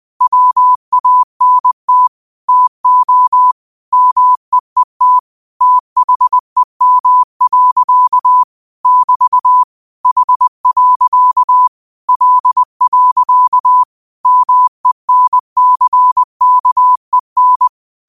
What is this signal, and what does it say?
Morse code, heard as a keyed tone, about 20 wpm, 1000 Hz. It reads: WANT TO MEET THEM. <BT> H. L. MENCKEN
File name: qotdfiles / 20wpm